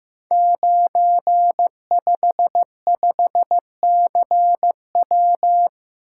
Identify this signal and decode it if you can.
Morse code, heard as a keyed tone, about 15 wpm, 700 Hz. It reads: 955CW
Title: Code Groups - 5 Letter:Numbers - 15wpm